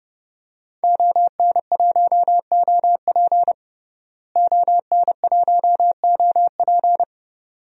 Morse code, heard as a keyed tone, about 30 wpm, 700 Hz. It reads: ON1OP ON1OP
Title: DXCC-100 Call Signs - 5 Characters - Repeated 2x - 30wpm